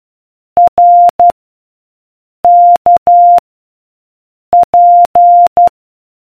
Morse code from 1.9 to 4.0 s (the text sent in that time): K